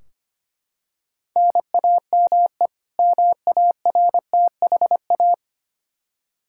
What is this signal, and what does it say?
Morse code, heard as a keyed tone, about 25 wpm, 700 Hz. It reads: NAME MARTHA
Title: Practice QSO - Exchange 1 - Beginning - 25wpm